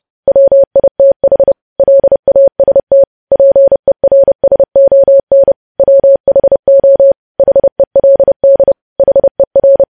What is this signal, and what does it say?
Morse code, heard as a keyed tone, about 30 wpm, 550 Hz. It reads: WITH LAST PERSON WHO HELD HER
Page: news headlines in morse code at 30 wpm